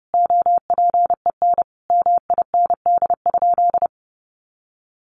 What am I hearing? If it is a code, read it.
Morse code, heard as a keyed tone, about 30 wpm, 700 Hz. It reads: OPEN MIND?
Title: Sentences from Top 1300 Words 30wpm